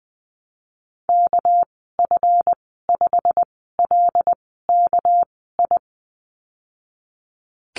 Morse code, heard as a keyed tone, about 19 wpm, spaced as for 14 wpm, 700 Hz. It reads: KF5LKI
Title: 213 Lesson 34g - Number 8 - Call Signs 20wpm